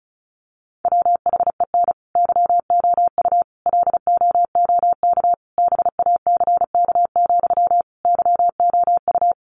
Morse code, heard as a keyed tone, about 34 wpm, 700 Hz. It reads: WHEN YOU LOOK BACK, YOU
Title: Sentences from Top 800 Words 35wpm